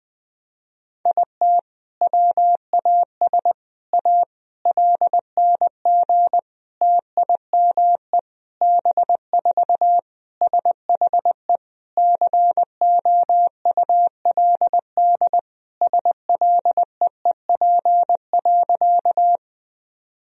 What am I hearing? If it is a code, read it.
Morse code, heard as a keyed tone, about 20 wpm, 700 Hz. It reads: IT WAS A LNG TIME B4 SHE COULD SLEEP.
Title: Sentences from Top 500 Words and Abbreviations 20wpm